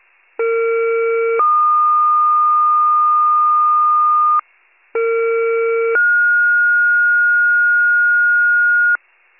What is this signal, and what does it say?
Two-tone page, heard as a one-second tone, then a three-second tone, about 460 then 1220 Hz, about 460 then 1430 Hz, with noise. It is heard from 0.4 to 9.0 s.